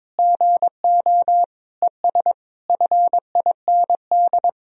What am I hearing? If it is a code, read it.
Morse code, heard as a keyed tone, about 22 wpm, 700 Hz. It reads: GO ES FIND
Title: Sentences from Top 500 Words and Abbreviations 22wpm